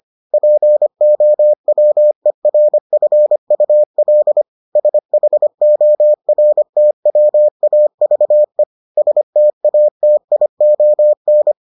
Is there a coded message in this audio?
Yes: Morse code from 0.3 to 11.5 s, heard as a keyed tone, about 25 wpm, 600 Hz. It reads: POWERFUL SHORTWAVE STATION